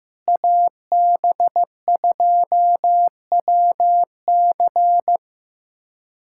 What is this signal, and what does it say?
Morse code, heard as a keyed tone, about 15 wpm, 700 Hz. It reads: AB2WC
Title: USA Callsign DE Callsign - 5 Characters - 15wpm